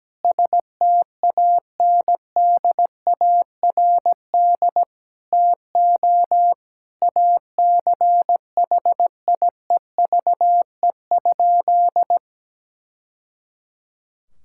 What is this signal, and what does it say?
Morse code, heard as a keyed tone, about 17 wpm, 700 Hz. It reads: STANDARD TO ACHIEVE?
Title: Sentences from Top 1000 Words - Review Part 6 - 17wpm